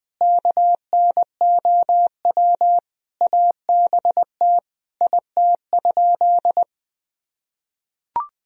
Morse code, heard as a keyed tone, about 20 wpm, 700 Hz. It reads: KNOW ABT IT?
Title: Sentences from Top 100 Words and Abbreviations 20wpm